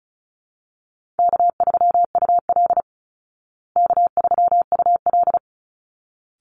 Morse code, heard as a keyed tone, about 34 wpm, 700 Hz. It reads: K3UL K3UL
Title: Easy Call Signs Repeated 2x - 35wpm